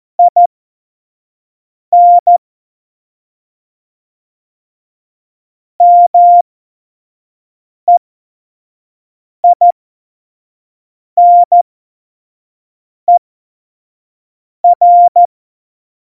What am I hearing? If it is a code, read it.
Morse code, heard as a keyed tone, about 14 wpm, spaced as for 5 wpm, 700 Hz. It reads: IN MEINER